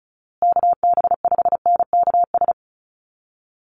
Morse code, heard as a keyed tone, about 34 wpm, 700 Hz. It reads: KB5NKS
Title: USA Call Signs Repeated 3x 35wpm